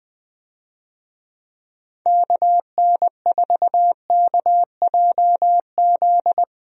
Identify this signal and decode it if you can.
Morse code, heard as a keyed tone, about 20 wpm, 700 Hz. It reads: KN4KJZ
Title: USA Call Signs - 6 Characters - Repeated 2x - 20wpm